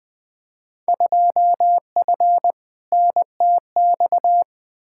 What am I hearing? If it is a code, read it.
Morse code, heard as a keyed tone, about 20 wpm, 700 Hz. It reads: 2F NTX
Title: Field Day Exchanges - 20wpm